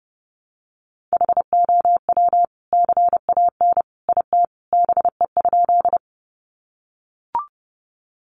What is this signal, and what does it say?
Morse code, heard as a keyed tone, about 30 wpm, 700 Hz. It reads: HOW CAN IT BE?
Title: Sentences from Top 100 Words 30wpm